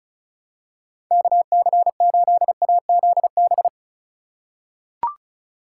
Morse code, heard as a keyed tone, about 36 wpm, 700 Hz. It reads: KC8AZB
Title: USA Call Signs - 6 Characters - 4x Character Spacing - 35wpm